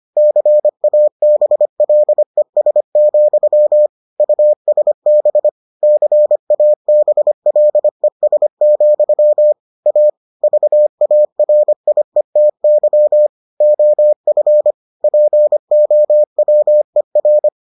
Morse code, heard as a keyed tone, about 25 wpm, 600 Hz. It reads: CABLES, USB CABLES, A VARIETY OF POWER